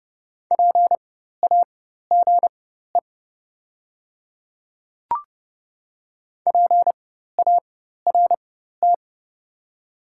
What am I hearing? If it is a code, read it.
Morse code, heard as a keyed tone, about 30 wpm, spaced as for 14 wpm, 700 Hz. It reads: PAGE PART
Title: 4x Character Spacing - Top 300 Words - 30wpm